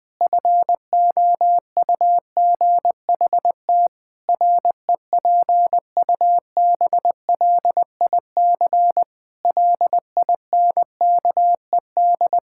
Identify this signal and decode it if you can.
Morse code, heard as a keyed tone, about 21 wpm, 700 Hz. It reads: FOUGHT REPUBLIC LINKED
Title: Pangram Challenge v2 - 20wpm